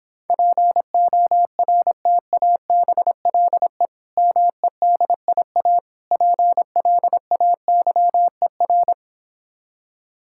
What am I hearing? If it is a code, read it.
Morse code, heard as a keyed tone, about 26 wpm, 700 Hz. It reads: PORTABLE MEDIA PLAYER